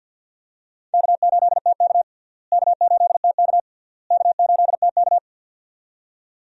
Morse code, heard as a keyed tone, about 51 wpm, 700 Hz. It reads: K8TK K8TK K8TK
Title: USA Call Signs - 4 Characters - Repeated 3x - 50wpm